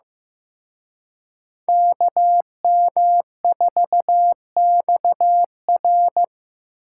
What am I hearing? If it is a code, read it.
Morse code, heard as a keyed tone, about 18 wpm, 700 Hz. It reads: KM4XR
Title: Speed Racing - USA 2x2 Call Signs 15wpm